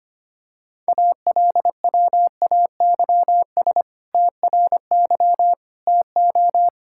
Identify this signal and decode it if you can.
Morse code, heard as a keyed tone, about 25 wpm, 700 Hz. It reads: ALWAYS TRY TO
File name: practice / 25wpm / Sentences with 2-Word Phrases - 25wpm